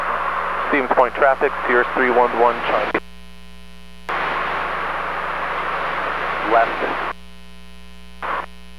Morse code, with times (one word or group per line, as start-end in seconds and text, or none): none